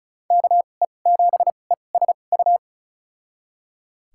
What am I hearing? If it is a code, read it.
Morse code, heard as a keyed tone, about 34 wpm, spaced as for 25 wpm, 700 Hz. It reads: KE7ESU
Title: USA Call Signs - 6 Characters - 2x Character Spacing - 35wpm